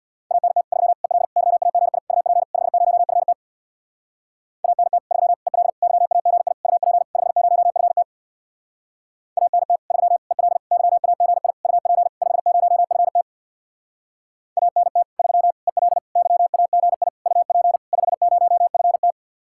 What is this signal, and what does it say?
Morse code, heard as a keyed tone, about 57 wpm, 700 Hz. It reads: ANT 3 EL YAGI UP 50FT ANT 3 EL YAGI UP 50FT ANT 3 EL YAGI UP 50FT ANT 3 EL YAGI UP 50FT
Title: Speed-Racing Common QSO Phrases Encore - 50wpm